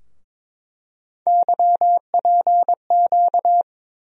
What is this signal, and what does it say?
Morse code, heard as a keyed tone, about 22 wpm, 700 Hz. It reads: YPQ
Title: USA Call Sign Postfixes - 3 Letters 22wpm